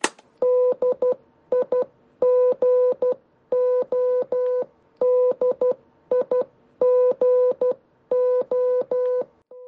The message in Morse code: DIGO DIGO